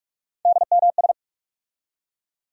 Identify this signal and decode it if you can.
Morse code, heard as a keyed tone, about 41 wpm, 700 Hz. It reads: DMS
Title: USA Call Sign Postfixes - 3 Letters 45wpm